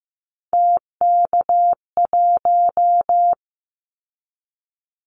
Morse code, heard as a keyed tone, about 15 wpm, 700 Hz. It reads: TK1
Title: Callsign Prefixes for Top 100 DXCC - 15wpm